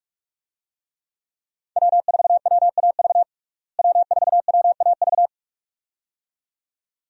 Morse code, heard as a keyed tone, about 43 wpm, 700 Hz. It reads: W4WAV W4WAV
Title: USA Call Signs - 5 Characters - Repeated 2x - 45wpm